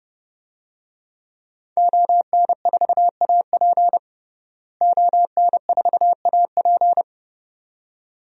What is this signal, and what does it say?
Morse code, heard as a keyed tone, about 30 wpm, 700 Hz. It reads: ON4AP ON4AP
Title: DXCC-100 Call Signs - 5 Characters - Repeated 2x - 30wpm